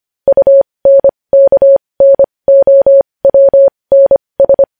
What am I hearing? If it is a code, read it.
Morse code, heard as a keyed tone, about 25 wpm, spaced as for 20 wpm, 550 Hz. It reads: UNKNOWNS
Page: news-headlines-in-morse-code-at-20-wpm